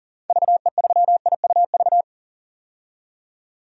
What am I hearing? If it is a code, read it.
Morse code, heard as a keyed tone, about 40 wpm, 700 Hz. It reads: VE3IUV
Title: DXCC-100 Call Signs - 6 Characters - 40wpm